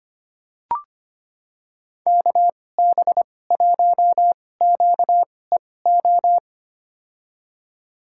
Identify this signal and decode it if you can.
Morse code, heard as a keyed tone, about 25 wpm, spaced as for 18 wpm, 700 Hz. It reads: KB1QEO
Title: USA Call Signs - 6 Characters - 2x Character Spacing - 25wpm